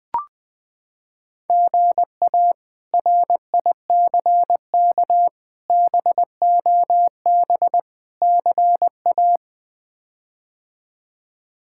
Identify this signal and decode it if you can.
Morse code, heard as a keyed tone, about 19 wpm, 700 Hz. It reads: GA RICK BOB CA
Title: K1USN SST QSO Practice - 20wpm